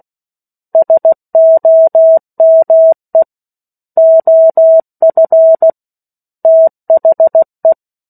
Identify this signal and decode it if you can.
Morse code, heard as a keyed tone, about 16 wpm, 650 Hz. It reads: SOME OF THE
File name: 03-Chapter-03 16 WPM